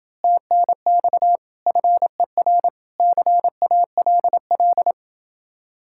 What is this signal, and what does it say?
Morse code, heard as a keyed tone, about 26 wpm, 700 Hz. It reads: TNX FER CALL